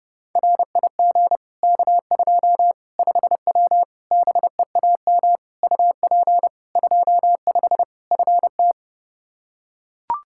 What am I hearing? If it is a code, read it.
Morse code, heard as a keyed tone, about 30 wpm, 700 Hz. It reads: RIG K2 5W BEAM UP 25 FT
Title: Practice QSO - Exchange 2 - Info - 30wpm